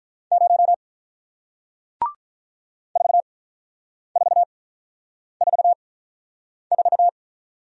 Morse code, heard as a keyed tone, about 47 wpm, spaced as for 9 wpm, 700 Hz. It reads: 0 4444